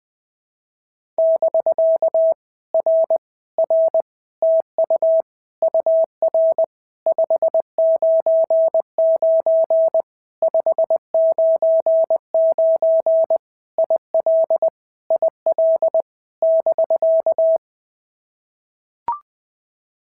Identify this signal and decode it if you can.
Morse code, heard as a keyed tone, about 20 wpm, 650 Hz. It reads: <BK> R R TU UR 599 599 IL IL <BK>
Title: POTA QSO Practice - 20wpm